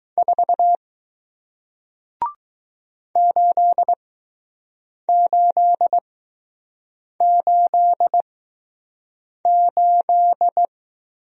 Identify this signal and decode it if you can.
Morse code, heard as a keyed tone, about 20 wpm, spaced as for 6 wpm, 700 Hz. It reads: 4 8888